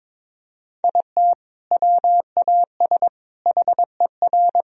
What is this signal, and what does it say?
Morse code, heard as a keyed tone, about 22 wpm, 700 Hz. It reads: IT WAS HER
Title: Sentences from Top 1000 Words - Review Part 4 - 22wpm